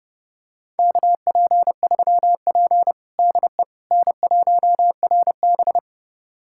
Morse code, heard as a keyed tone, about 32 wpm, 700 Hz. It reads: KP3P DE N1RB
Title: USA Callsign DE Callsign - 4 Characters - 30wpm